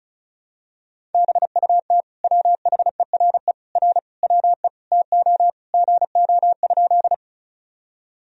Morse code, heard as a keyed tone, about 34 wpm, 690 Hz. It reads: BUT WHERE R WE TO GO?
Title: Sentences from Top 200 Words and Abbreviations 35wpm